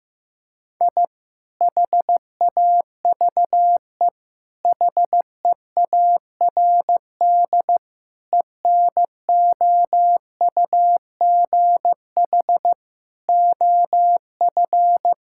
Morse code, heard as a keyed tone, about 15 wpm, 700 Hz. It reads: I HAVE HEARD ENOUGH OF
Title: Sentences from Top 1000 Words - Review Part 6 - 15wpm